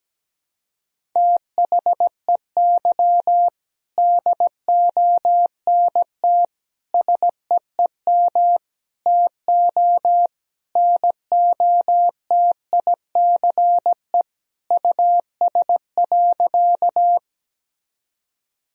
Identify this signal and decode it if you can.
Morse code, heard as a keyed tone, about 17 wpm, 700 Hz. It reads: THEY DONT SEEM TO NOTICE US.
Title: Sentences from Top 1000 Words - Review Part 2 - 17wpm